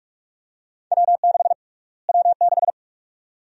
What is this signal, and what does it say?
Morse code, heard as a keyed tone, about 47 wpm, 700 Hz. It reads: W6 W6
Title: USA Call Sign Prefixes - Repeated 2x - 45wpm